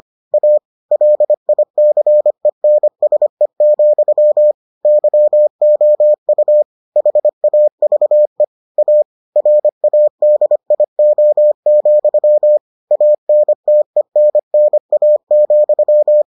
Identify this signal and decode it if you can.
Morse code, heard as a keyed tone, about 25 wpm, 600 Hz. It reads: A LICENSE, YOU HAVE A RADIO, ANTENNA,